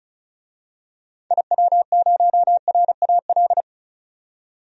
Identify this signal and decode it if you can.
Morse code, heard as a keyed tone, about 34 wpm, 700 Hz. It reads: IW0RAL
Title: DXCC-100 Call Signs - 6 Characters - 35wpm